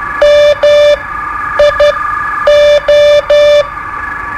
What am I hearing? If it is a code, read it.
Morse code, heard as a keyed tone, about 12 wpm, spaced as for 9 wpm, 590 Hz. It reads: MIO
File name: qso-01